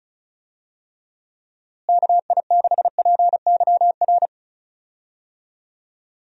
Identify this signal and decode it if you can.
Morse code, heard as a keyed tone, about 36 wpm, 700 Hz. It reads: KI6PYR
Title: USA Call Signs - 6 Characters - Repeated 2x - 35wpm